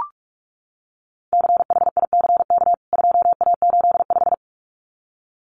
Morse code, heard as a keyed tone, about 44 wpm, 700 Hz. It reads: CHICK 2A85
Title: CWT Exchange 45wpm